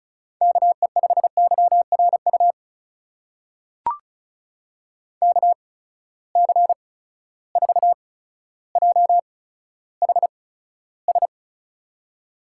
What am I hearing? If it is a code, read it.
Morse code, heard as a keyed tone, about 36 wpm, spaced as for 15 wpm, 700 Hz. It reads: KE5YRU KC4JHS
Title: USA Call Signs - 6 Characters - 8x Character Spacing - 35wpm